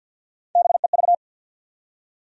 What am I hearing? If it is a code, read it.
Morse code, heard as a keyed tone, about 52 wpm, 700 Hz. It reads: BEV
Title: USA Common Names - Max 4 Letters - 50wpm